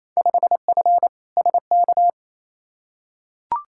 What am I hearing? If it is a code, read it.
Morse code, heard as a keyed tone, about 28 wpm, 700 Hz. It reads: 5F SK
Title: Field Day Exchanges Encore - 28wpm